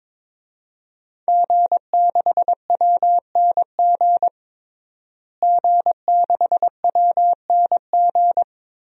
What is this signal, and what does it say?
Morse code, heard as a keyed tone, about 21 wpm, 700 Hz. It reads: G6WNG G6WNG
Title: DXCC-100 Call Signs - 5 Characters - Repeated 3x - 22wpm